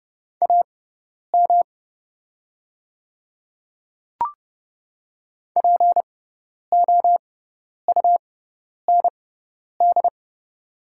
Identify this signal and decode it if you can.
Morse code, heard as a keyed tone, about 30 wpm, spaced as for 10 wpm, 690 Hz. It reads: AM POUND
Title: 6x Character Spacing - Top 500 Words - 30wpm